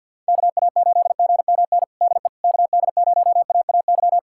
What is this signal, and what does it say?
Morse code, heard as a keyed tone, about 48 wpm, 700 Hz. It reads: KA8GMN DE KD0AAY
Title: USA Callsign DE Callsign 50wpm